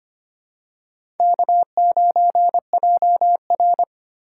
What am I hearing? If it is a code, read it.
Morse code, heard as a keyed tone, about 25 wpm, 700 Hz. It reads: K9JR
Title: USA Call Signs - 4 Characters - 4x Character Spacing - 25wpm